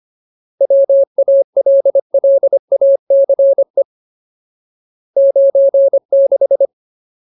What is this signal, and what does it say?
Morse code, heard as a keyed tone, about 25 wpm, 550 Hz. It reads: WALLACE 96
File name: Ultimate Word List / 25WPM Ultimate